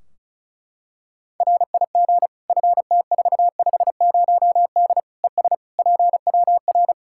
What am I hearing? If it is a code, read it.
Morse code, heard as a keyed tone, about 36 wpm, 700 Hz. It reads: RIG FT450D ES PWR
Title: Practice QSO - Exchange 2 - Info - 35wpm